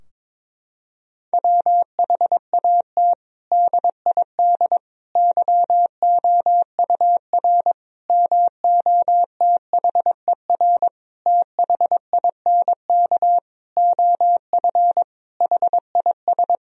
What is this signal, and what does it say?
Morse code, heard as a keyed tone, about 22 wpm, 700 Hz. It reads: WHAT DID YOUR MOTHER THINK OF HIS
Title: Sentences from Top 1000 Words - Review Part 4 - 22wpm